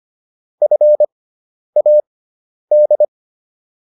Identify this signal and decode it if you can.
Morse code, heard as a keyed tone, about 25 wpm, spaced as for 10 wpm, 600 Hz. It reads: FAD